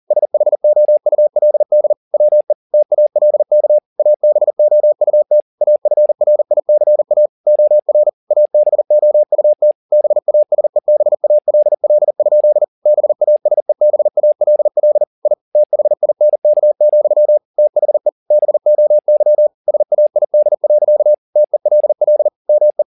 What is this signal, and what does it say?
Morse code, heard as a keyed tone, about 40 wpm, 600 Hz. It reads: SHOULD WE TALK ABOUT AFRICA OR ABOUT BASEBALL? BASEBALL I THINK, THE BOY SAID. TELL ME